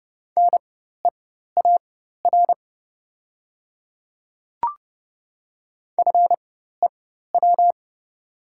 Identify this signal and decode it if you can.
Morse code, heard as a keyed tone, about 30 wpm, spaced as for 14 wpm, 700 Hz. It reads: NEAR FEW
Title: 4x Character Spacing - Top 300 Words - 30wpm